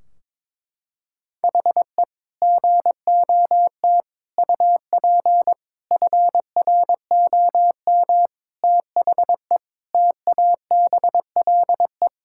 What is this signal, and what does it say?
Morse code, heard as a keyed tone, about 22 wpm, 700 Hz. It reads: HE GOT UP FROM THE TABLE
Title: Sentences from Top 1600 Words 22wpm